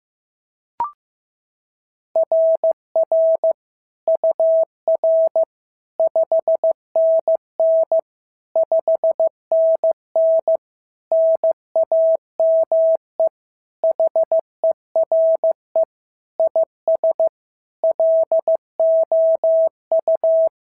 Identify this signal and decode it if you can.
Morse code, heard as a keyed tone, about 14 wpm, 650 Hz. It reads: RR UR 5NN 5NN NAME HERE IS LOU
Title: SOTA QSO Practice 15wpm